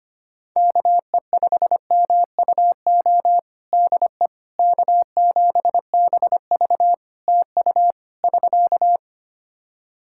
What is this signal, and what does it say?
Morse code, heard as a keyed tone, about 25 wpm, 700 Hz. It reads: KE5MUO DE K7BV TU <SK>
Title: K1USN SST QSO Practice 25wpm